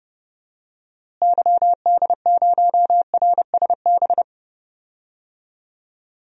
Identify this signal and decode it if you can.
Morse code, heard as a keyed tone, about 30 wpm, 700 Hz. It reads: YD0RSB
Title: DXCC-100 Call Signs - 6 Characters - Repeated 3x - 30wpm